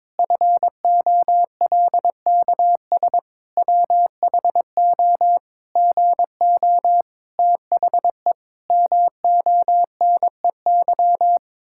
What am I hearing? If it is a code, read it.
Morse code, heard as a keyed tone, about 22 wpm, 700 Hz. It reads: FOLKS WHO GO THE MONEY
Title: Pangram Challenge v2 - 22wpm